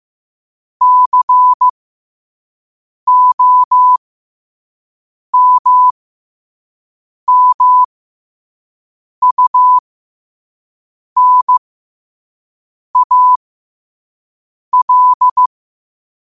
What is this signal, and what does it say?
Morse code, heard as a keyed tone, about 15 wpm, spaced as for 5 wpm, 1000 Hz. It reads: COMMUNAL
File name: qotdfiles / 5wpm